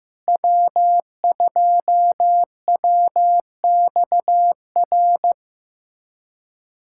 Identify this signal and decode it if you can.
Morse code, heard as a keyed tone, about 15 wpm, 700 Hz. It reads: W2WXR
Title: USA Callsign DE Callsign - 5 Characters - 15wpm